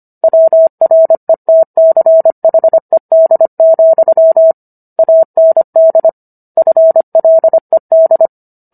Morse code, heard as a keyed tone, about 25 wpm, 650 Hz. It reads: WRETCHED, AND FLED